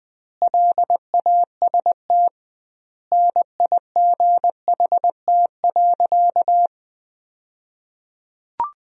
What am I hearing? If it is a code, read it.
Morse code, heard as a keyed tone, about 21 wpm, 700 Hz. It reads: LAST NIGHT.
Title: 2x Word-Spacing - Sentences from Top 300 Words - 20wpm